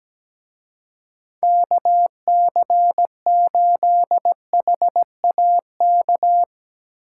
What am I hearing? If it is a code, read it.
Morse code, heard as a keyed tone, about 19 wpm, 700 Hz. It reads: KC8HAK
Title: Speed Racing - USA Call Signs 17wpm